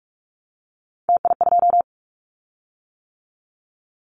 Morse code, heard as a keyed tone, about 44 wpm, 700 Hz. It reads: TI2
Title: Callsign Prefixes for Top 100 DXCC - 45wpm